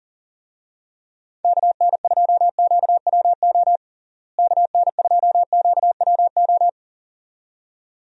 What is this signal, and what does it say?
Morse code, heard as a keyed tone, about 37 wpm, 700 Hz. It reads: KN2QWO KN2QWO
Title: USA Call Signs - 6 Characters - Repeated 2x - 40wpm